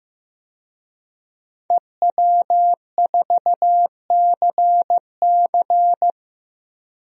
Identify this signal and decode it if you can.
Morse code, heard as a keyed tone, about 15 wpm, 700 Hz. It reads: EW4CC
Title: DXCC-100 Call Signs - 5 Characters - Repeated 3x - 15wpm